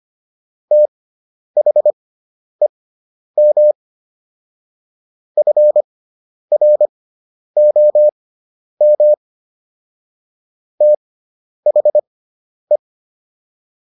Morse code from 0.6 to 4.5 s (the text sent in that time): THEM